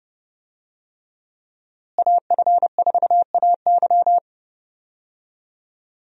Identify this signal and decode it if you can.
Morse code, heard as a keyed tone, about 30 wpm, 700 Hz. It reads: AF4AY
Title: USA Call Signs - 5 Characters - Repeated 3x - 30wpm